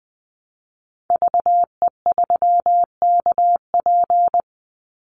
Morse code, heard as a keyed tone, about 20 wpm, 700 Hz. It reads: VE3KP
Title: Call Signs - 20wpm